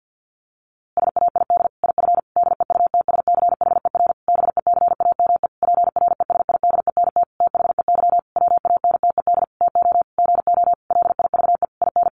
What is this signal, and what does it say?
Morse code, heard as a keyed tone, about 50 wpm, 700 Hz. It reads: SAID IF DEUTSCHER BECAME PRESIDENT THEY WANTED TO GO LIVE IN
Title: A Sound Of Thunder 50wpm